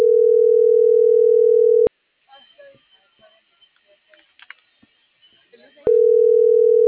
Ringback tone (North America), 440 + 480 Hz.